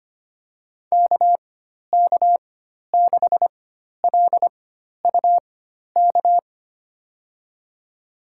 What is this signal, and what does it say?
Morse code, heard as a keyed tone, about 25 wpm, spaced as for 12 wpm, 700 Hz. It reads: KK6LUK
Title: USA Call Signs - 6 Characters - 4x Character Spacing - 25wpm